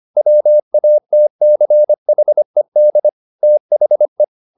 Morse code, heard as a keyed tone, about 25 wpm, 600 Hz. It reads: WATCHED THE
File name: the old man and the sea part 1 25wpm